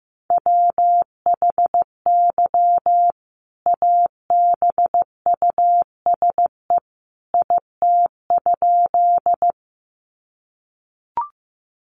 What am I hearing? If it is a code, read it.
Morse code, heard as a keyed tone, about 15 wpm, 700 Hz. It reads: WHY ABUSE IT?
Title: Sentences from Top 1300 Words 15wpm